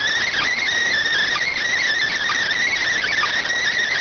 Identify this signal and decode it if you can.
Emergency Alert System (SAME) tones, heard as harsh buzzing bursts. It is heard from 0.3 to 3.1 s.